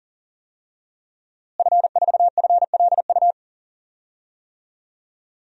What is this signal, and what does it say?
Morse code, heard as a keyed tone, about 40 wpm, 700 Hz. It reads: F4FLU
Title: DXCC-100 Call Signs - 5 Characters - Repeated 3x - 40wpm